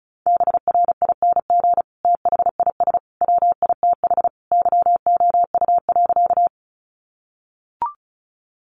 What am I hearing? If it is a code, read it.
Morse code, heard as a keyed tone, about 34 wpm, 700 Hz. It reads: BRING THIS WITH YOU.
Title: Sentences from Top 500 Words 35wpm